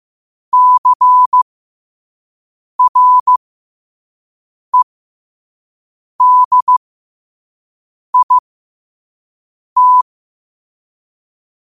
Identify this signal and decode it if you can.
Morse code, heard as a keyed tone, about 15 wpm, spaced as for 5 wpm, 1000 Hz. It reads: CREDIT
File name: qotdfiles / 5wpm